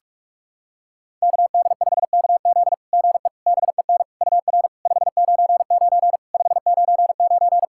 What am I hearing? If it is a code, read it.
Morse code, heard as a keyed tone, about 46 wpm, 690 Hz. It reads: KD5KZ GE BEN UR 599 599
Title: SOTA QSO Practice 45wpm